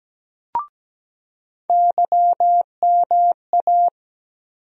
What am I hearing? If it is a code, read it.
Morse code, heard as a keyed tone, about 17 wpm, 700 Hz. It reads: YMA
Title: USA Call Sign Postfixes - 3 Letters 17wpm